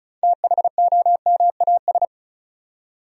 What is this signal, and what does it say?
Morse code, heard as a keyed tone, about 36 wpm, 700 Hz. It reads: THOMAS
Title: USA Common Names - 35wpm